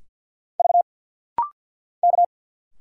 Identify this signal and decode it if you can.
Morse code, heard as a keyed tone, about 47 wpm, 690 Hz. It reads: V K
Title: Single Letters - Rapid-Fire 50wpm